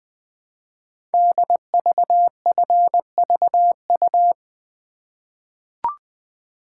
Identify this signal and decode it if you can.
Morse code, heard as a keyed tone, about 20 wpm, 700 Hz. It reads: DVFVU
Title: Code Groups - 5 Letters - 20wpm